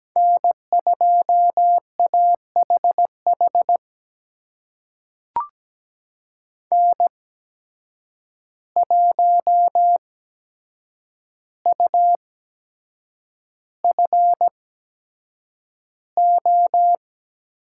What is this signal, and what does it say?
Morse code, heard as a keyed tone, about 17 wpm, spaced as for 7 wpm, 700 Hz. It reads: N2AHH N1UFO
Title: USA Call Signs - 5 Characters - 8x Character Spacing - 17wpm